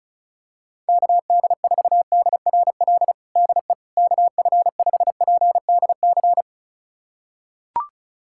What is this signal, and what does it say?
Morse code, heard as a keyed tone, about 34 wpm, 700 Hz. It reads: KD4DRL DE KF5PDC
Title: USA Callsign DE Callsign - 6 Characters - 35wpm